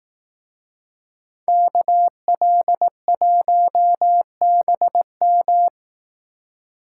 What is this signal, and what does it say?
Morse code, heard as a keyed tone, about 18 wpm, 700 Hz. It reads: KL1BM
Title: Speed Racing - USA 2x2 Call Signs 15wpm